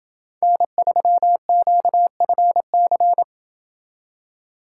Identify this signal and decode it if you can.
Morse code, heard as a keyed tone, about 27 wpm, 700 Hz. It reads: N3QFC